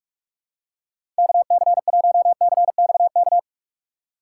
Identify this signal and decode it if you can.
Morse code, heard as a keyed tone, about 41 wpm, 700 Hz. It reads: KC1CXK
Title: USA Call Signs - 6 Characters - 6x Character Spacing - 45wpm